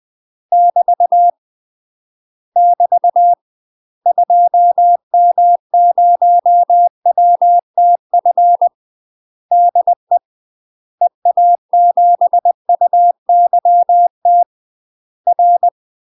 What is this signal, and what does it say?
Morse code, heard as a keyed tone, about 20 wpm, 700 Hz. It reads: <BT> <BT> 2M0WTF DE EA7UYT R
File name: UK MASTER QSOs TRAINING / 20WPM / UK MASTER QSOs 0046